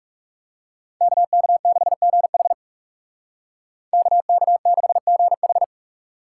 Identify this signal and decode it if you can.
Morse code, heard as a keyed tone, about 37 wpm, 700 Hz. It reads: KK6GH KK6GH
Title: Speed Racing - USA 2x2 Call Signs 30wpm